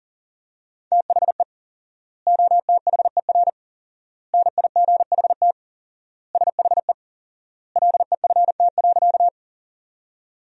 Morse code, heard as a keyed tone, about 40 wpm, 700 Hz. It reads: THE OTHER NIGHT SHE LEFT.
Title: 4x Word-Spacing - Sentences from Top 300 Words - 40wpm